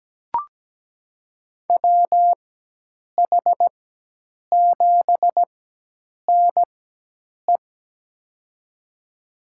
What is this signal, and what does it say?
Morse code, heard as a keyed tone, about 17 wpm, spaced as for 8 wpm, 700 Hz. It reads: WH7NE
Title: USA Call Signs - 4x Character Spacing - 17wpm